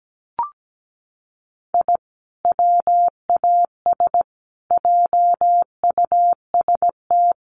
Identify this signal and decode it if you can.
Morse code, heard as a keyed tone, about 17 wpm, 700 Hz. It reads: I WAS JUST
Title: Sets of 5 Words Encore 17wpm